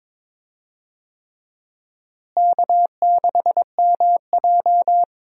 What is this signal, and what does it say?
Morse code, heard as a keyed tone, about 22 wpm, 700 Hz. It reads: K6MJ
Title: USA Call Signs - 4 Characters - Repeated 3x - 22wpm